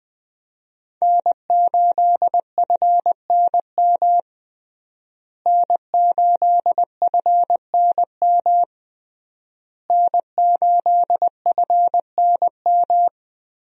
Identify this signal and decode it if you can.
Morse code, heard as a keyed tone, about 21 wpm, 700 Hz. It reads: N8FNM N8FNM N8FNM
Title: USA Call Signs - 5 Characters - Repeated 3x - 20wpm